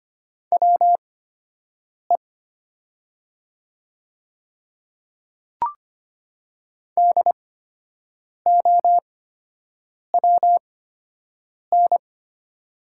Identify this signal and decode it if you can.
Morse code, heard as a keyed tone, about 25 wpm, spaced as for 7 wpm, 690 Hz. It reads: WE DOWN
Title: 8x Character Spacing - Top 200 Words - 25wpm